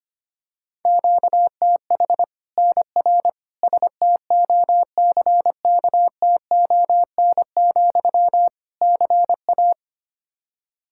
Morse code, heard as a keyed tone, about 25 wpm, 700 Hz. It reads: QTH NR STOCKTON, CA
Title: Practice QSO - Exchange 1 - Beginning - 25wpm